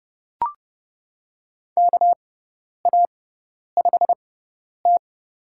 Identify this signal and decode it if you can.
Morse code, heard as a keyed tone, about 31 wpm, spaced as for 10 wpm, 700 Hz. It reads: KA5T
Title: USA Call Signs - 4 Characters - 6x Character Spacing - 30wpm